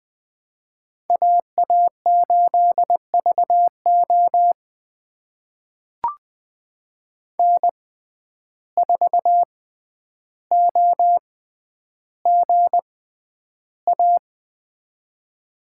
Morse code, heard as a keyed tone, about 20 wpm, spaced as for 10 wpm, 700 Hz. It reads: AA8VO N4OGA
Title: USA Call Signs - 5 Characters - 6x Character Spacing - 20wpm